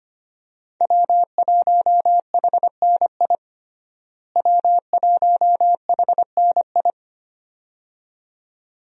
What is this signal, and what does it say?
Morse code, heard as a keyed tone, about 25 wpm, 700 Hz. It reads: W1HNI W1HNI
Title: USA Call Signs - 5 Characters - Repeated 2x - 25wpm